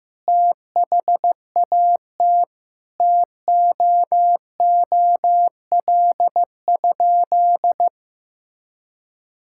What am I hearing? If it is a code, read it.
Morse code, heard as a keyed tone, about 15 wpm, 700 Hz. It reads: THAT TOOL?
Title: Sentences from Top 1000 Words - Review Part 6 - 15wpm